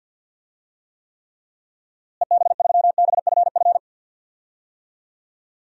Morse code, heard as a keyed tone, about 52 wpm, 700 Hz. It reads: EB3BFF
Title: DXCC-100 Call Signs - 6 Characters - Repeated 3x - 50wpm